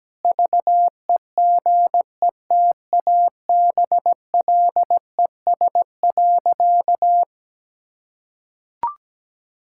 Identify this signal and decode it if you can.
Morse code, heard as a keyed tone, about 17 wpm, 700 Hz. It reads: VEGETABLES.
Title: Pangram Challenge v2 - 17wpm